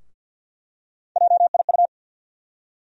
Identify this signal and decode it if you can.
Morse code, heard as a keyed tone, about 48 wpm, 700 Hz. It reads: JIU
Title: USA Call Sign Postfixes - 3 Letters 50wpm